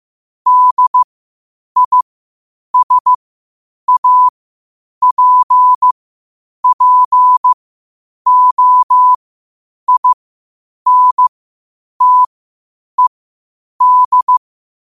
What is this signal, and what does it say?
Morse code, heard as a keyed tone, about 15 wpm, spaced as for 8 wpm, 1000 Hz. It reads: DISAPPOINTED
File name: qotdfiles / 8wpm